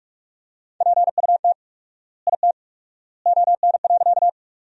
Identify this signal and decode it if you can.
Morse code, heard as a keyed tone, about 46 wpm, 700 Hz. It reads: PUT IT ON.
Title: 4x Word-Spacing - Sentences from Top 200 Words - 45wpm